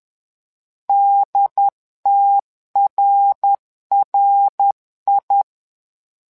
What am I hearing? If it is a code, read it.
Morse code, heard as a keyed tone, about 10 wpm, 800 Hz. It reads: DTRRI
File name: LEZIONI / 3 MWRO / 3 MWRO 10wpm 2148